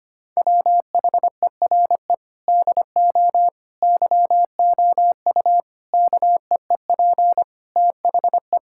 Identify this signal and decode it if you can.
Morse code, heard as a keyed tone, about 25 wpm, 700 Hz. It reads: WHERE DO YOU KEEP THE
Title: Sentences from Top 1000 Words - Review Part 5 - 25wpm